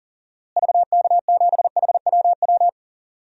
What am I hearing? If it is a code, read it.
Morse code, heard as a keyed tone, about 39 wpm, 700 Hz. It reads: VK7HWW
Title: DXCC-100 Call Signs - 6 Characters - Repeated 3x - 40wpm